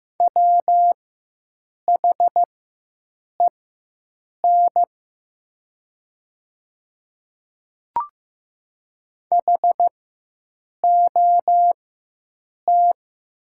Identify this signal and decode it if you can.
Morse code, heard as a keyed tone, about 15 wpm, spaced as for 7 wpm, 700 Hz. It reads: WHEN HOT
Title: 4x Character Spacing - Top 100 Words - 15wpm